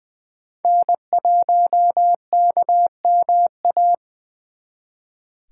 Morse code, heard as a keyed tone, about 25 wpm, 700 Hz. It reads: N1KMA
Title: Speed Racing - USA Call Signs 20wpm